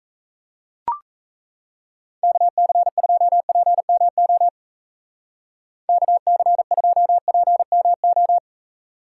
Morse code, heard as a keyed tone, about 35 wpm, 700 Hz. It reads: KC2PMO KC2PMO